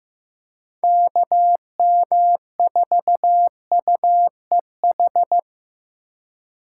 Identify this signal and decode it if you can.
Morse code, heard as a keyed tone, about 15 wpm, 700 Hz. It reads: KM4UEH
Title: USA Call Signs - 6 Characters - 4x Character Spacing - 15wpm